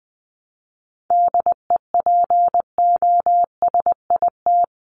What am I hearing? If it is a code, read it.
Morse code, heard as a keyed tone, about 20 wpm, 700 Hz. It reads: DEPOSIT
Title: Top 1251-1500 Words 20wpm